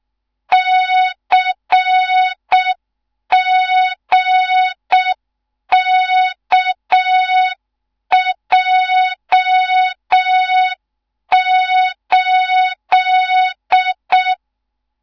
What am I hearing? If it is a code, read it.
Morse code, heard as a keyed tone, about 6 wpm, 3700 Hz. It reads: CGKJ8
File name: Morze-HA2MN / hang / 15 GYAKORLAT